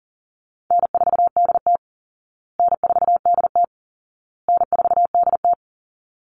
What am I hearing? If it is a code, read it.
Morse code, heard as a keyed tone, about 40 wpm, 700 Hz. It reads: N4DT N4DT N4DT
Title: Call Signs Repeated 2x - 40wpm